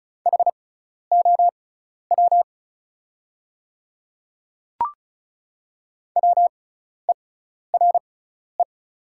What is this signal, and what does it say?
Morse code, heard as a keyed tone, about 36 wpm, spaced as for 12 wpm, 700 Hz. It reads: HOW WERE
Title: 6x Character Spacing - Top 100 Words - 35wpm